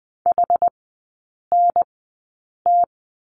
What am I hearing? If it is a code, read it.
Morse code, heard as a keyed tone, about 20 wpm, spaced as for 8 wpm, 700 Hz. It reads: HNT
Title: 107 Lesson 18e - Letter m - Three Characters 20wpm